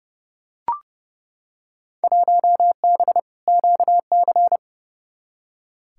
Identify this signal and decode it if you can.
Morse code, heard as a keyed tone, about 32 wpm, 700 Hz. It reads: 1B QC
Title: Field Day Exchanges Encore - 30wpm